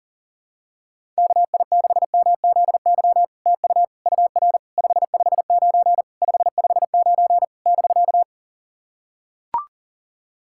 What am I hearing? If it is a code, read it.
Morse code, heard as a keyed tone, about 40 wpm, 700 Hz. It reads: KI6MZY TU UR 559 559 <BK>
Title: SOTA QSO Practice 40wpm